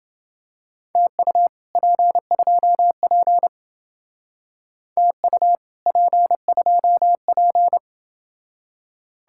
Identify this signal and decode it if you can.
Morse code, heard as a keyed tone, about 27 wpm, 700 Hz. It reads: TU P2P TU P2P